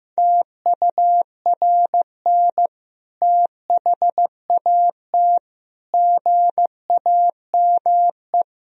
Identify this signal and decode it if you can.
Morse code, heard as a keyed tone, about 14 wpm, 700 Hz. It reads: TURN THAT GAME
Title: Sentences from Top 1000 Words - Review Part 3 - 15wpm